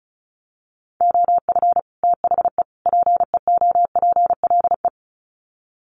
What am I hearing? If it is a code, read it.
Morse code, heard as a keyed tone, about 34 wpm, 700 Hz. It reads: OF THE PEOPLE
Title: Sets of 3 Words 35wpm